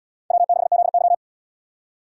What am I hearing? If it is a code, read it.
Morse code, heard as a keyed tone, about 71 wpm, 700 Hz. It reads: W6ZY